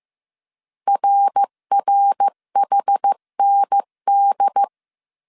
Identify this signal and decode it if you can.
Morse code, heard as a keyed tone, about 15 wpm, 800 Hz. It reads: RRHND
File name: CW / DET / 1 3 1500C S A 14wpm 2447